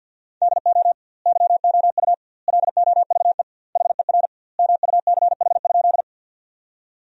Morse code, heard as a keyed tone, about 50 wpm, 700 Hz. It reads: DO YOU LOVE HER MUCH?
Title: 2x Word-Spacing - Sentences from Top 400 Words - 50wpm